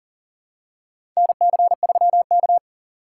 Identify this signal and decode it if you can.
Morse code, heard as a keyed tone, about 37 wpm, 700 Hz. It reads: NC3K
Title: USA Call Signs - 4 Characters - 3x Character Spacing - 40wpm